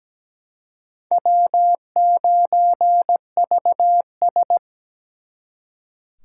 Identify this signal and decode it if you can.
Morse code, heard as a keyed tone, about 20 wpm, 700 Hz. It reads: W9VS
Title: Speed Racing - USA 1x2 Call Signs 17wpm